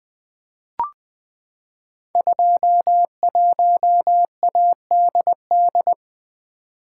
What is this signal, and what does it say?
Morse code, heard as a keyed tone, about 20 wpm, 700 Hz. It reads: 21ADD
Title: Code Groups - 5 Letter:Numbers - 20wpm